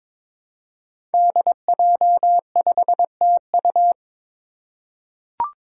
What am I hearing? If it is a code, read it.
Morse code, heard as a keyed tone, about 22 wpm, 700 Hz. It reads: DJ5TU
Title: Five Challenging Letter-Number Combinations 22wpm